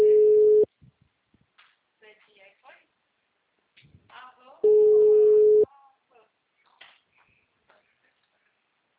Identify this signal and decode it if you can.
Ringback tone (Europe, 425 Hz), with speech.